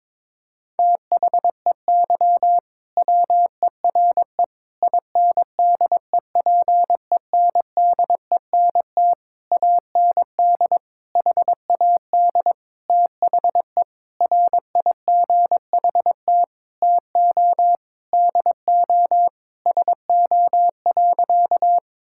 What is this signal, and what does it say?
Morse code, heard as a keyed tone, about 22 wpm, 700 Hz. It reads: THEY WERE INDEPENDENT AND HAD THE RIGHT TO DO SO.
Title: Sentences from Top 1000 Words - Review Part 6 - 22wpm